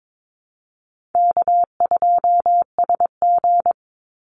Morse code, heard as a keyed tone, about 22 wpm, 700 Hz. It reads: K2SG
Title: Easy Call Signs - 22wpm